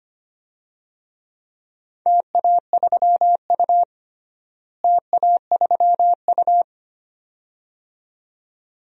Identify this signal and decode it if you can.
Morse code, heard as a keyed tone, about 25 wpm, 700 Hz. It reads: TA3U TA3U
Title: DXCC-100 Call Signs - 4 Characters - Repeated 2x - 25wpm